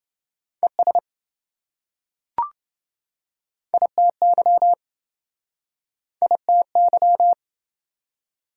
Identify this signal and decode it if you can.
Morse code, heard as a keyed tone, about 28 wpm, 700 Hz. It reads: ES ITY ITY